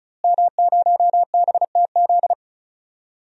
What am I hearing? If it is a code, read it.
Morse code, heard as a keyed tone, about 34 wpm, 700 Hz. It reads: M0BTZ
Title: DXCC-100 Call Signs - 5 Characters - Repeated 3x - 35wpm